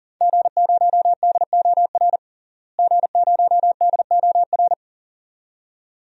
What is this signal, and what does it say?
Morse code, heard as a keyed tone, about 37 wpm, 700 Hz. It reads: G0DOR G0DOR
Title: DXCC-100 Call Signs - 5 Characters - Repeated 2x - 40wpm